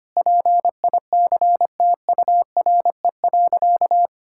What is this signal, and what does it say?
Morse code, heard as a keyed tone, about 25 wpm, 700 Hz. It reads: PICTURE.
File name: practice / 25wpm / Sentences with 4-Word Phrases - 25wpm